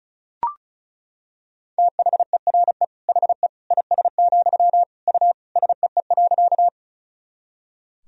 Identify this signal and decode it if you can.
Morse code, heard as a keyed tone, about 36 wpm, 700 Hz. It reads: THERE HE IS, U SEE.
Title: Sentences from Top 100 Words and Abbreviations 35wpm